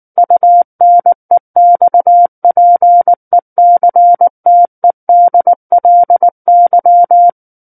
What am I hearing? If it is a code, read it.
Morse code, heard as a keyed tone, about 19 wpm, 700 Hz. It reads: UNEXPECTEDLY
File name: Book One / 10-Chapter 10 - 19 WPM